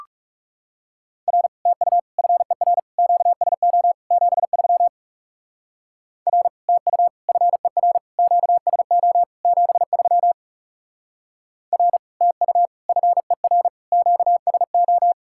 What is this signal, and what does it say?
Morse code, heard as a keyed tone, about 35 wpm, 700 Hz. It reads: R TU FER QSO 73 R TU FER QSO 73 R TU FER QSO
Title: Speed-Racing Common QSO Phrases Encore - 30wpm